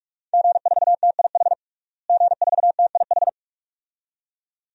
Morse code, heard as a keyed tone, about 46 wpm, 700 Hz. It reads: G4TIH G4TIH
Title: DXCC-100 Call Signs - 5 Characters - Repeated 2x - 45wpm